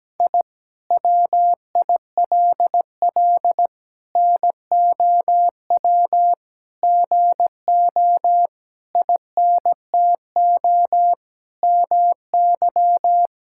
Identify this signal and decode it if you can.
Morse code, heard as a keyed tone, about 17 wpm, 700 Hz. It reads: I WILL NOW GO INTO MY
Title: Sentences from Top 1000 Words - Review Part 6 - 17wpm